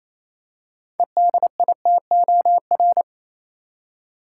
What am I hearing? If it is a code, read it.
Morse code, heard as a keyed tone, about 28 wpm, 700 Hz. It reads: EDITOR
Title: Six Letter Words - 28wpm